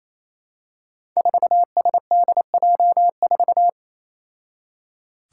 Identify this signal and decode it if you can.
Morse code, heard as a keyed tone, about 28 wpm, 700 Hz. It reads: 4SDJ4
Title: Five Challenging Letter-Number Combinations 28wpm